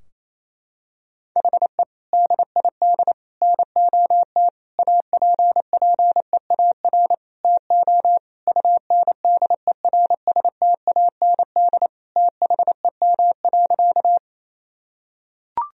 Morse code, heard as a keyed tone, about 28 wpm, 700 Hz. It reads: HE DID NOT APPEAR TO UNDERSTAND THEM.
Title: Sentences from Top 1000 Words - Review Part 2 - 28wpm